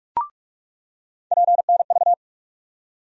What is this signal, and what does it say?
Morse code, heard as a keyed tone, about 46 wpm, 700 Hz. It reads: PNV
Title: USA Call Sign Postfixes - 3 Letters 45wpm